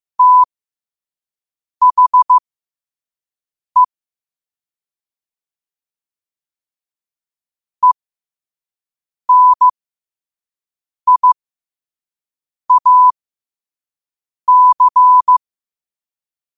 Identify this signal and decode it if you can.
Morse code, heard as a keyed tone, about 15 wpm, spaced as for 5 wpm, 1000 Hz. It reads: THE ENIAC
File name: qotdfiles / 5wpm